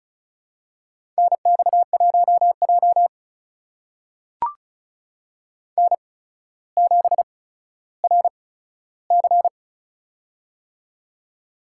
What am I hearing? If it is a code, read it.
Morse code, heard as a keyed tone, about 36 wpm, spaced as for 15 wpm, 700 Hz. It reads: NX1J N7RC
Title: USA Call Signs - 4 Characters - 8x Character Spacing - 35wpm